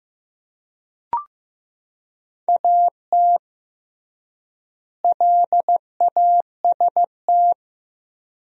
Morse code, heard as a keyed tone, about 15 wpm, 700 Hz. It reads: AT LAST
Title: 3x Word-Spacing - Sentences from Top 300 Words - 15wpm